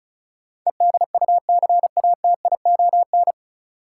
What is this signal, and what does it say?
Morse code, heard as a keyed tone, about 34 wpm, 700 Hz. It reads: EDUCATION
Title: Sentences from Top 1100 Words - New Word Drill Alt - 35wpm